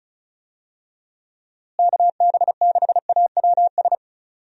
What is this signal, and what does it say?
Morse code, heard as a keyed tone, about 34 wpm, 700 Hz. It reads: KB6AWS
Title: USA Call Signs - 6 Characters - Repeated 3x - 35wpm